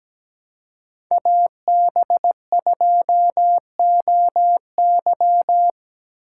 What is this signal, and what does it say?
Morse code, heard as a keyed tone, about 20 wpm, 700 Hz. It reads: AB2OY
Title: Speed Racing - USA 2x2 Call Signs 17wpm